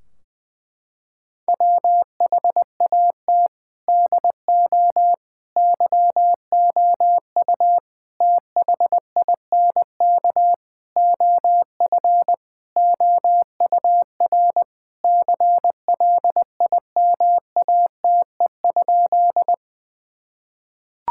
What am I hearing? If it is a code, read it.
Morse code, heard as a keyed tone, about 20 wpm, 700 Hz. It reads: WHAT DO YOU THINK OF OUR CLIMATE?
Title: Sentences from Top 1600 Words 20wpm